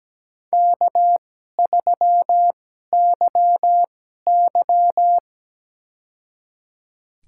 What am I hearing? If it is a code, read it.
Morse code, heard as a keyed tone, about 17 wpm, spaced as for 12 wpm, 700 Hz. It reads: K3YY
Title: USA Call Signs - 4 Characters - 2x Character Spacing - 17wpm